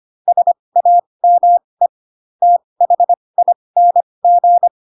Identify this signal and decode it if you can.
Morse code, heard as a keyed tone, about 25 wpm, spaced as for 20 wpm, 700 Hz. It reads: SAME THING
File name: Pinocchio / Pinocchio ch4 15 25 ik1hge